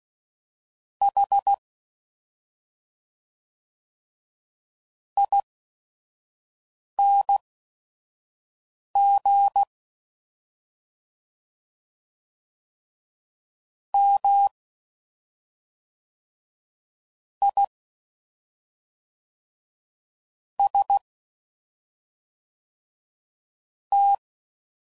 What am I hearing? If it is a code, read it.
Morse code, heard as a keyed tone, about 16 wpm, spaced as for 5 wpm, 800 Hz. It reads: H ING M I S T